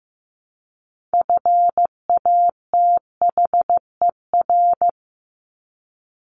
Morse code, heard as a keyed tone, about 15 wpm, 700 Hz. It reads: FATHER
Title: Top 200 Words Repeated 15wpm